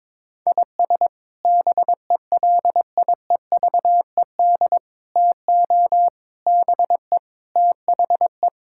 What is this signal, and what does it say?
Morse code, heard as a keyed tone, about 22 wpm, 700 Hz. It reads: IS BELIEVED TO BE THE
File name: practice / 22wpm / Sets of 6 Words - 22wpm